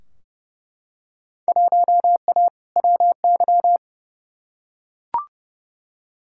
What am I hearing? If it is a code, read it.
Morse code, heard as a keyed tone, about 30 wpm, 700 Hz. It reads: 1A WY
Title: Field Day Exchanges Encore - 30wpm